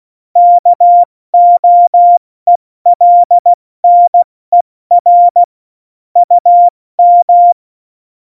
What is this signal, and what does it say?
Morse code, heard as a keyed tone, about 16 wpm, spaced as for 14 wpm, 700 Hz. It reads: KOELNER UM
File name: Morsekurs / Witze / Witz 22. Dezember 2020, 70 F80